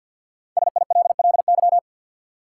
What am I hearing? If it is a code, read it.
Morse code, heard as a keyed tone, about 52 wpm, 700 Hz. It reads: SILLY
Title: Five Letter Words - 50wpm